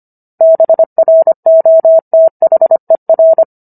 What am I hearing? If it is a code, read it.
Morse code, heard as a keyed tone, about 25 wpm, 650 Hz. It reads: BROTHER